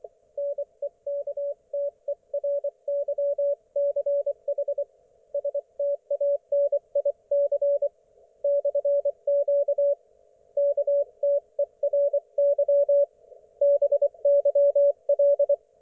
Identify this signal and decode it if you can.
Morse code, heard as a keyed tone, about 24 wpm, 570 Hz. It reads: NEKTERYCH STANIC /Q KTERY BYL
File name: Archiv / CUC224 / ok1fhi cul224